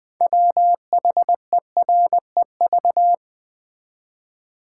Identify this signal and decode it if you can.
Morse code, heard as a keyed tone, about 20 wpm, 700 Hz. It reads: WHEREV
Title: Build A Word - 2 Letters At A Time - 8-Letter Words - 20wpm